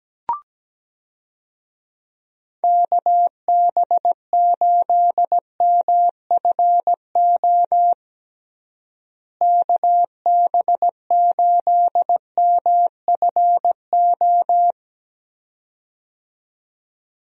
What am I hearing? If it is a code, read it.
Morse code, heard as a keyed tone, about 17 wpm, 700 Hz. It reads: KB8MFO KB8MFO
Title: USA Call Signs - 6 Characters - Repeated 2x - 17wpm